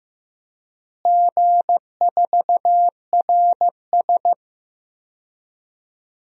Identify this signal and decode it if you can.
Morse code, heard as a keyed tone, about 15 wpm, 700 Hz. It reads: G4RS
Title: DXCC-100 Call Signs - 4 Characters - Repeated 2x - 15wpm